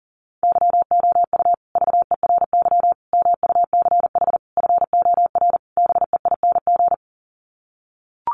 Morse code, heard as a keyed tone, about 40 wpm, 700 Hz. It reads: YOU VERY MUCH FOR BEING
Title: Sets of 5 Words Encore 40wpm